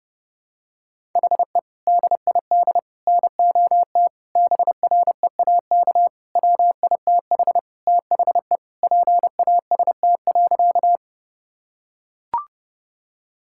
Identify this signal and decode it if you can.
Morse code, heard as a keyed tone, about 32 wpm, 700 Hz. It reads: HE DID NOT BREAK WITH THE PAST.
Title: Sentences from Top 1000 Words - Review Part 6 - 30wpm